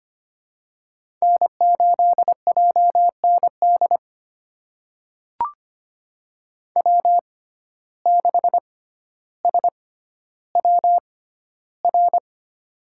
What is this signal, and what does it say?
Morse code, heard as a keyed tone, about 25 wpm, spaced as for 13 wpm, 700 Hz. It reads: N8JND W6SWR
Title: USA Call Signs - 5 Characters - 6x Character Spacing - 25wpm